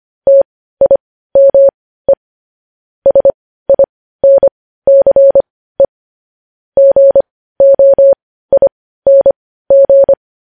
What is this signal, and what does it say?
Morse code, heard as a keyed tone, about 25 wpm, spaced as for 15 wpm, 550 Hz. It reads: TIME SINCE GOING
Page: news headlines in morse code at 15 wpm